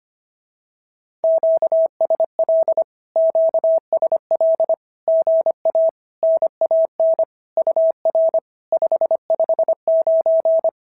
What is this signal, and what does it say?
Morse code, heard as a keyed tone, about 25 wpm, 650 Hz. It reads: QSL QSL GA NAN UR 559
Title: SOTA QSO Practice 25wpm